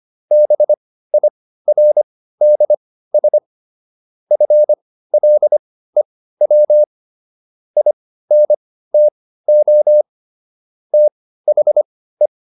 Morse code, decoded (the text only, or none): BIRDS FLEW INTO THE